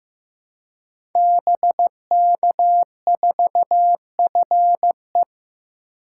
Morse code, heard as a keyed tone, about 15 wpm, 700 Hz. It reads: BK4FE
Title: Code Groups - 5 Letter:Numbers - 15wpm